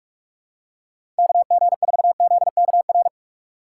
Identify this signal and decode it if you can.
Morse code, heard as a keyed tone, about 46 wpm, 700 Hz. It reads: KG4ZKR
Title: USA Call Signs - 6 Characters - 2x Character Spacing - 45wpm